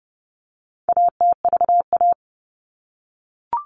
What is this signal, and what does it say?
Morse code, heard as a keyed tone, about 30 wpm, 700 Hz. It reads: ATVA
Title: All Cut Numbers - 4-Digit 30wpm